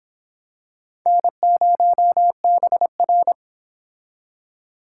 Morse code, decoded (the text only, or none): N0BR